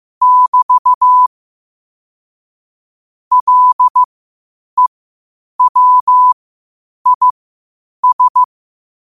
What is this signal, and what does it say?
Morse code, heard as a keyed tone, about 15 wpm, spaced as for 8 wpm, 1000 Hz. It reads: <BT> LEWIS
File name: qotdfiles / 8wpm